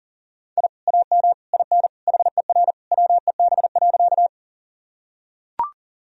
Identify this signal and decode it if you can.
Morse code, heard as a keyed tone, about 40 wpm, 700 Hz. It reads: I AM IN HER WEB.
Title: Sentences from Top 1000 Words - Review Part 4 - 40wpm